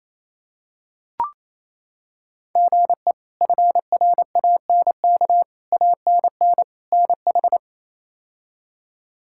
Morse code, heard as a keyed tone, about 28 wpm, 700 Hz. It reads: GE FRANK ANN NH
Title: K1USN SST QSO Practice - 28wpm